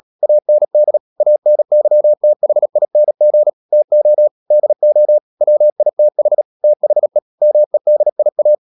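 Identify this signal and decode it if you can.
Morse code, heard as a keyed tone, about 37 wpm, 600 Hz. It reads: AND ANYTHING TO DO WITH THE MEDIA
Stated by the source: Morse code practice at 35–39 wpm sending English text of Ken Follett's novel Never, from CW med Gnister.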